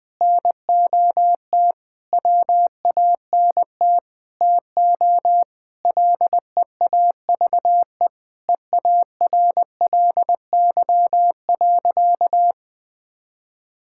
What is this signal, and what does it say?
Morse code, heard as a keyed tone, about 21 wpm, 700 Hz. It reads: NOT WANT TO LEAVE EARLY.
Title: Sentences with 5-Word Phrases - 20wpm